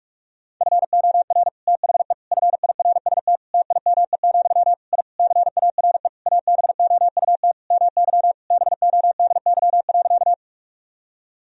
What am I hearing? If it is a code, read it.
Morse code, heard as a keyed tone, about 46 wpm, 700 Hz. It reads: FOR THE FIRST TIME, I CARE ABOUT MY BODY.
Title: Sentences from Top 1000 Words - Review Part 2 - 45wpm